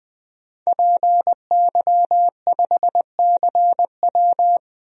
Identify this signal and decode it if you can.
Morse code, heard as a keyed tone, about 20 wpm, 700 Hz. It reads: PY5CW
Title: DXCC-100 Call Signs - 5 Characters - 20wpm